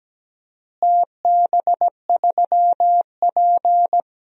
Morse code, heard as a keyed tone, about 17 wpm, 700 Hz. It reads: TB3P
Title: DXCC-100 Call Signs - 4 Characters - Repeated 3x - 17wpm